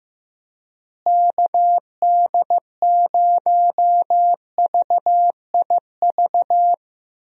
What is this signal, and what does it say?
Morse code, heard as a keyed tone, about 15 wpm, 700 Hz. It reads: KD0VIV
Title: USA Call Signs - 6 Characters - Repeated 3x - 15wpm